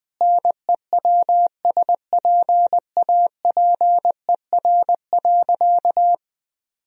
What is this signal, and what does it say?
Morse code, heard as a keyed tone, about 19 wpm, 700 Hz. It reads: NEWSPAPER.
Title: Sentences from Top 1000 Words - Review Part 4 - 20wpm